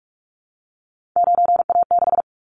Morse code, heard as a keyed tone, about 46 wpm, 700 Hz. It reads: 9A6
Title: Callsign Prefixes for Top 100 DXCC - 45wpm